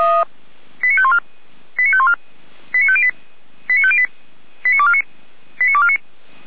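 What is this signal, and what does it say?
Five-tone selective call (ZVEI), heard as quick runs of five tones, tones of about 70 ms. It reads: 89514 89514 89508 89508 89249 89249
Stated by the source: ZVEI five-tone selective call 89558 received by SDR.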